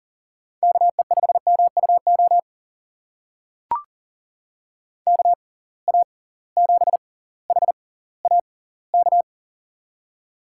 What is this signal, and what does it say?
Morse code, heard as a keyed tone, about 39 wpm, spaced as for 20 wpm, 700 Hz. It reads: KE5MUO KA7HAK
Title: USA Call Signs - 6 Characters - 6x Character Spacing - 40wpm